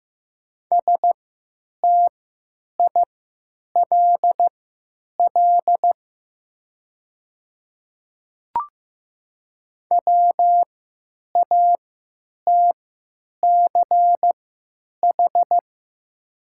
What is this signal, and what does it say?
Morse code, heard as a keyed tone, about 15 wpm, spaced as for 9 wpm, 700 Hz. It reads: STILL WATCH
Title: 3x Character Spacing - Top 500 Words - 15wpm